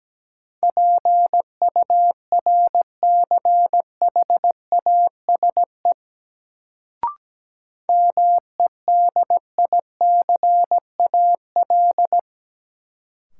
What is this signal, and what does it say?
Morse code, heard as a keyed tone, about 17 wpm, 700 Hz. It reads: PURCHASE MEDICAL
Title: Sentences from Top 800 Words - New Word Drill - 17wpm